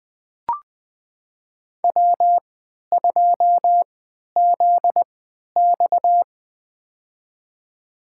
Morse code, heard as a keyed tone, about 21 wpm, spaced as for 11 wpm, 700 Hz. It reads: W2ZX
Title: USA Call Signs - 3x Character Spacing - 20wpm